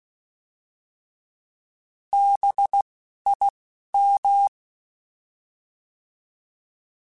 Morse code, heard as a keyed tone, about 16 wpm, spaced as for 11 wpm, 800 Hz. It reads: BIM
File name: Alphabetical / Alphabetical mp3s / Bs @10